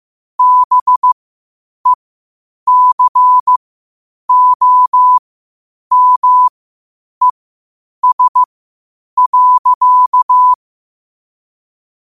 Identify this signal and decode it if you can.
Morse code, heard as a keyed tone, about 15 wpm, spaced as for 8 wpm, 1000 Hz. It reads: BECOMES.
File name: qotdfiles / 8wpm